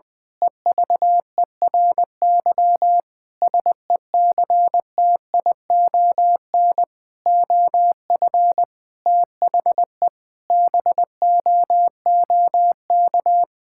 Morse code, heard as a keyed tone, about 21 wpm, 700 Hz. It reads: EVERY SECTION OF THE BOOK
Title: Sentences from Top 1000 Words - Review Part 5 - 20wpm